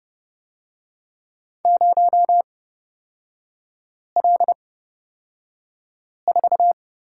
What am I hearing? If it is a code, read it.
Morse code, heard as a keyed tone, about 31 wpm, spaced as for 5 wpm, 690 Hz. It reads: 0L4
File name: Lesson 37e - Letter x - Three Characters 30wpm